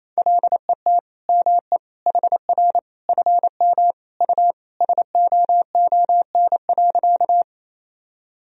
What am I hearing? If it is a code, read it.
Morse code, heard as a keyed tone, about 28 wpm, 700 Hz. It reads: LET ME HR FM U SOON.
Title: Sentences from Top 400 Words and Abbreviations 28wpm